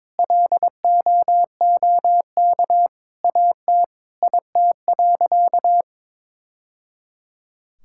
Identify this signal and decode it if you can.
Morse code, heard as a keyed tone, about 22 wpm, 700 Hz. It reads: LOOK AT IT.
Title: Sentences from Top 1000 Words - Review Part 5 - 22wpm